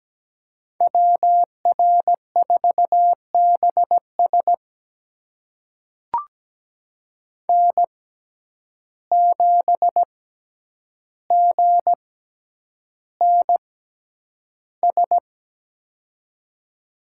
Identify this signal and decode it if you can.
Morse code, heard as a keyed tone, about 17 wpm, spaced as for 9 wpm, 700 Hz. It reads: WR4BS N7GNS
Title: USA Call Signs - 5 Characters - 6x Character Spacing - 17wpm